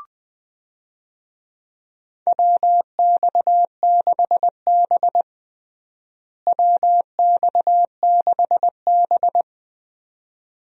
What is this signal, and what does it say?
Morse code, heard as a keyed tone, about 20 wpm, 700 Hz. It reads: WX6B WX6B
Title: USA Call Signs - 4 Characters - Repeated 2x - 20wpm